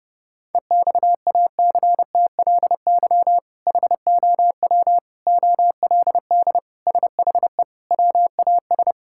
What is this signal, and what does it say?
Morse code, heard as a keyed tone, about 31 wpm, 700 Hz. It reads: EXACTLY HOW OLD SHE WAS
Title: Sentences from Top 1600 Words 30wpm